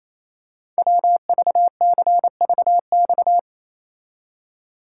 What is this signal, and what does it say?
Morse code, heard as a keyed tone, about 28 wpm, 700 Hz. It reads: WVCVX
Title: Code Groups - 5 Letter:Numbers - 28wpm